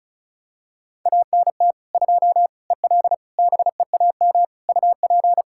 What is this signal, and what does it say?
Morse code, heard as a keyed tone, about 36 wpm, 700 Hz. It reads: ANT 2 EL BEAM UP
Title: K1USN SST QSO Practice 35wpm